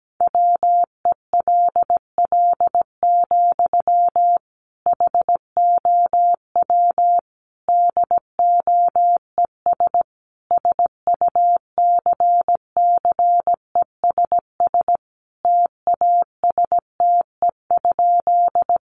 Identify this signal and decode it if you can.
Morse code, heard as a keyed tone, about 17 wpm, 700 Hz. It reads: WELL, HOW DOES SUCCESS TASTE?
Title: Sentences from Top 1100 Words 17wpm